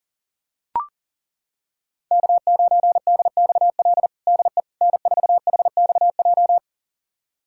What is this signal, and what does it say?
Morse code, heard as a keyed tone, about 40 wpm, 700 Hz. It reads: K9DXL DE N4HXJ
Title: USA Callsign DE Callsign - 5 Characters - 40wpm